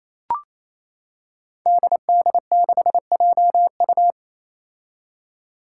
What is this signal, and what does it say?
Morse code, heard as a keyed tone, about 28 wpm, 700 Hz. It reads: DD6JU
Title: Five Challenging Letter-Number Combinations 28wpm